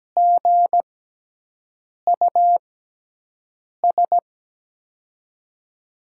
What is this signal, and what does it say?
Morse code, heard as a keyed tone, about 17 wpm, spaced as for 6 wpm, 700 Hz. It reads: GUS
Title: USA Call Signs - 5 Characters - 6x Character Spacing - 17wpm